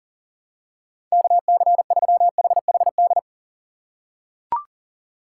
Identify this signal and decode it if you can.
Morse code, heard as a keyed tone, about 40 wpm, 700 Hz. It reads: KC3HHD
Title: USA Call Signs - 6 Characters - 8x Character Spacing - 40wpm